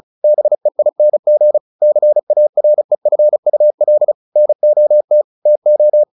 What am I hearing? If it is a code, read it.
Morse code, heard as a keyed tone, about 34 wpm, 600 Hz. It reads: BEING CAREFUL NOT TO
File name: the old man and the sea / the old man and the sea 1 35wpm